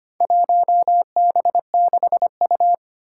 Morse code, heard as a keyed tone, about 25 wpm, 700 Hz. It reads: 1B6U
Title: Four Challenging Letter-Number Combinations 25wpm